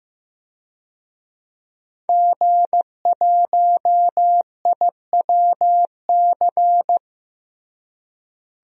Morse code, heard as a keyed tone, about 15 wpm, 700 Hz. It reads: G1IWC
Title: DXCC-100 Call Signs - 5 Characters - Repeated 3x - 15wpm